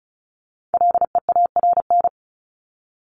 Morse code, heard as a keyed tone, about 34 wpm, 700 Hz. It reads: LEARN
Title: ING Suffix - Top 100 Words with Base Words 35wpm